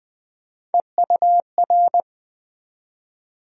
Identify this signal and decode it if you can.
Morse code, heard as a keyed tone, about 20 wpm, 700 Hz. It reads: EUR
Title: USA Call Sign Postfixes - 3 Letters 20wpm